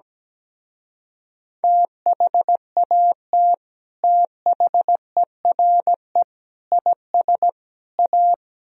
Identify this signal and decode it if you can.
Morse code, heard as a keyed tone, about 18 wpm, 700 Hz. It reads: THAT THERE IS A
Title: Speed-Racing - Sets of 4 Words 17wpm